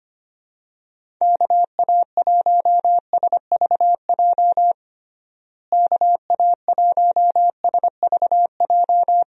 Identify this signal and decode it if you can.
Morse code, heard as a keyed tone, about 25 wpm, 700 Hz. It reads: KA1SVJ KA1SVJ
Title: USA Call Signs - 6 Characters - Repeated 2x - 25wpm